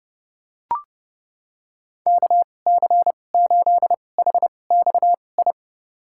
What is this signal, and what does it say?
Morse code, heard as a keyed tone, about 32 wpm, spaced as for 22 wpm, 700 Hz. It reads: KC8HXI
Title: USA Call Signs - 6 Characters - 2x Character Spacing - 30wpm